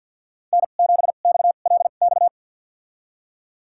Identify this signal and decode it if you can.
Morse code, heard as a keyed tone, about 52 wpm, spaced as for 37 wpm, 700 Hz. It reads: N7XLX
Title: USA Call Signs - 2x Character Spacing - 50wpm